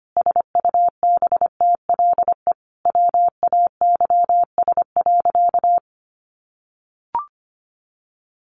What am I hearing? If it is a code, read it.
Morse code, heard as a keyed tone, about 25 wpm, 700 Hz. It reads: SUBTLE WAYS.
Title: A Sound Of Thunder 25wpm